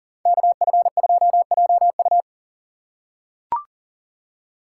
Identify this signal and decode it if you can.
Morse code, heard as a keyed tone, about 40 wpm, 700 Hz. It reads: KF2JU
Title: USA Call Signs - 5 Characters - 4x Character Spacing - 40wpm